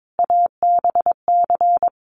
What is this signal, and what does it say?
Morse code, heard as a keyed tone, about 22 wpm, 700 Hz. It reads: ABC
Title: Sets of 5 Words 22wpm